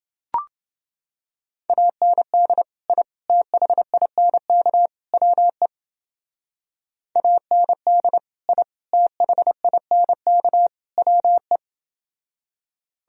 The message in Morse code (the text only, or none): AND I THINK WE AND I THINK WE